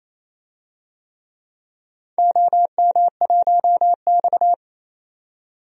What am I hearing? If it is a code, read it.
Morse code, heard as a keyed tone, about 27 wpm, 700 Hz. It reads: OM1X
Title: DXCC-100 Call Signs - 4 Characters - 28wpm